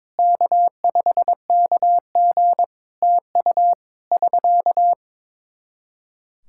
Morse code, heard as a keyed tone, about 22 wpm, 700 Hz. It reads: K5KG TU <SK>
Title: Practice QSO - Exchange 4 - Ending - 22wpm